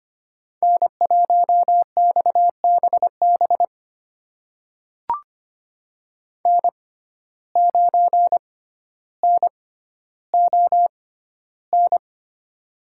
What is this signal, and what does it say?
Morse code, heard as a keyed tone, about 25 wpm, spaced as for 13 wpm, 700 Hz. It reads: N1XBB N9NON
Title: USA Call Signs - 5 Characters - 6x Character Spacing - 25wpm